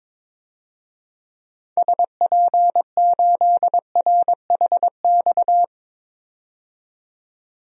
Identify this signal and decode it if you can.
Morse code, heard as a keyed tone, about 22 wpm, 700 Hz. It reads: SP8RHX
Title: DXCC-100 Call Signs - 6 Characters - 22wpm